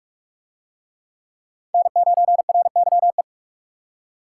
Morse code, heard as a keyed tone, about 39 wpm, 700 Hz. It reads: N9RYE